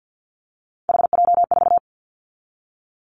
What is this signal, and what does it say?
Morse code, heard as a keyed tone, about 50 wpm, 690 Hz. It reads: HJ4
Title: Callsign Prefixes for Top 100 DXCC - 50wpm